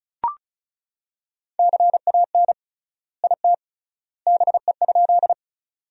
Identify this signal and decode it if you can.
Morse code, heard as a keyed tone, about 36 wpm, 700 Hz. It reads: CAN IT BE?
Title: 3x Word-Spacing - Sentences from Top 100 Words - 35wpm